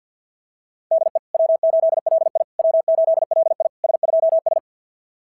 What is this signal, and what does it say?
Morse code, heard as a keyed tone, about 53 wpm, 690 Hz. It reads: DE W8LI W8LI S2S
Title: SOTA QSO Practice 50wpm